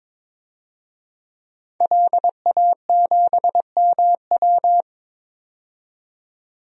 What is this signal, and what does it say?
Morse code, heard as a keyed tone, about 22 wpm, 700 Hz. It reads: LA7MW
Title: DXCC-100 Call Signs - 5 Characters - 22wpm